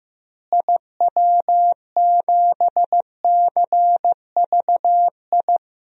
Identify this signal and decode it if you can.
Morse code, heard as a keyed tone, about 15 wpm, 700 Hz. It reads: IW7CVI
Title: DXCC-100 Call Signs - 6 Characters - Repeated 3x - 15wpm